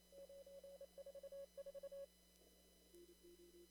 Morse code, heard as a keyed tone, about 27 wpm, 560 Hz. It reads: 944
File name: CW / kt5j wpx cw-op-k5pi